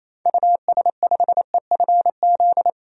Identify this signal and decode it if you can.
Morse code, heard as a keyed tone, about 28 wpm, 700 Hz. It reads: US5EFZ
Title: DXCC-100 Call Signs - 6 Characters - 28wpm